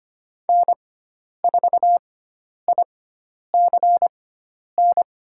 Morse code, heard as a keyed tone, about 25 wpm, spaced as for 10 wpm, 700 Hz. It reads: N4ICN
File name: Lesson 16g - Letter c - Call Signs 25wpm